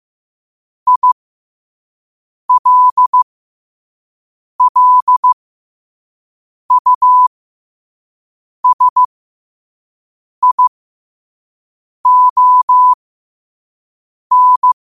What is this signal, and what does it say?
Morse code, heard as a keyed tone, about 15 wpm, spaced as for 5 wpm, 1000 Hz. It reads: ILLUSION
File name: qotdfiles / 5wpm